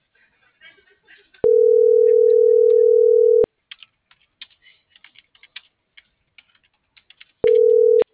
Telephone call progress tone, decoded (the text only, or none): ringback tone